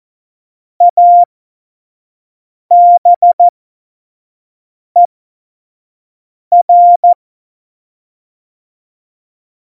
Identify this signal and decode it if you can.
Morse code, heard as a keyed tone, about 14 wpm, spaced as for 5 wpm, 700 Hz. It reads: ABER
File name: Morsekurs / Witze / Witz 29. Dezember 2020, 25 F70